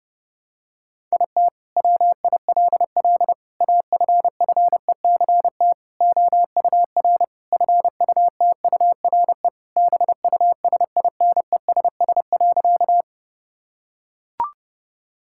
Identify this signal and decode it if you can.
Morse code, heard as a keyed tone, about 32 wpm, 700 Hz. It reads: IT WILL AFFECT OUR FUTURE BUSINESS.
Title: Sentences from Top 1000 Words - Review Part 6 - 30wpm